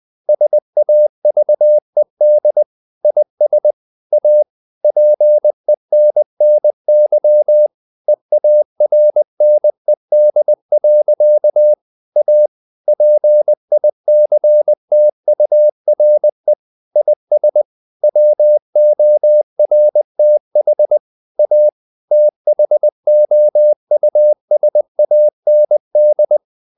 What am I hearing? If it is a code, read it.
Morse code, heard as a keyed tone, about 20 wpm, 600 Hz. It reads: SAVED IS A PENNY EARNED. A PICTURE IS WORTH A THOUSAND
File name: CW / Practice / EnglishSayings20